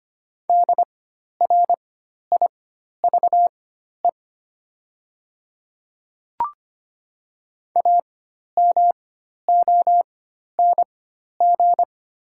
Morse code, heard as a keyed tone, about 25 wpm, spaced as for 12 wpm, 700 Hz. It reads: DRIVE AMONG
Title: 4x Character Spacing - Top 500 Words - 25wpm